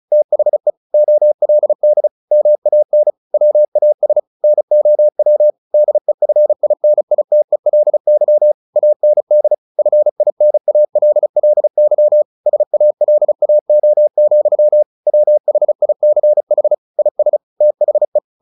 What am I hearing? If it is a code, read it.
Morse code, heard as a keyed tone, about 34 wpm, 600 Hz. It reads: THE OLD MAN WAS NOW DEFINITELY AND FINALLY SALAO, WHICH IS THE
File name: the old man and the sea / the old man and the sea 1 35wpm